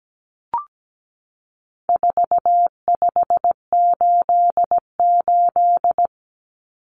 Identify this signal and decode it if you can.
Morse code, heard as a keyed tone, about 17 wpm, 700 Hz. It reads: 4588
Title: Common Cut Numbers - 4-Digit 17wpm